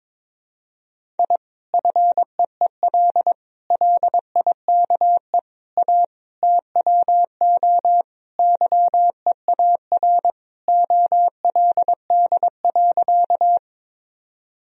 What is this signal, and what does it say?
Morse code, heard as a keyed tone, about 22 wpm, 700 Hz. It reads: I FEEL LIKE A TWO YEAR OLD.
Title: Sentences from Top 1000 Words - Review Part 1 - 22wpm